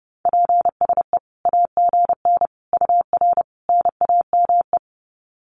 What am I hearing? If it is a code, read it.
Morse code, heard as a keyed tone, about 30 wpm, 700 Hz. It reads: PSE AGN UR NAME
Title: Common QSO Phrases 30wpm